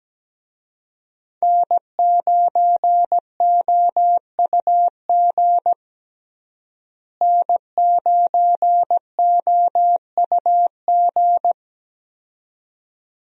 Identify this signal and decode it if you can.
Morse code, heard as a keyed tone, about 17 wpm, 700 Hz. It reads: N9OUG N9OUG
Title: USA Call Signs - 5 Characters - Repeated 2x - 17wpm